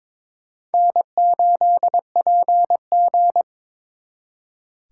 Morse code, heard as a keyed tone, about 26 wpm, 700 Hz. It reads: N8PG
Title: Speed Racing - USA 1x2 Call Signs 22wpm